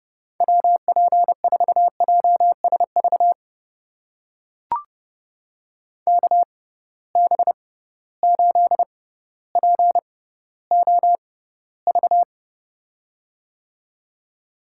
Morse code, heard as a keyed tone, about 30 wpm, spaced as for 15 wpm, 700 Hz. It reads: WP4JSV KB8POV
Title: USA Call Signs - 6 Characters - 6x Character Spacing - 30wpm